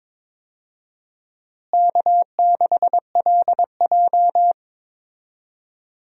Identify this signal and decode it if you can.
Morse code, heard as a keyed tone, about 22 wpm, 700 Hz. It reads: K6LJ
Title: USA Call Signs - 4 Characters - Repeated 3x - 22wpm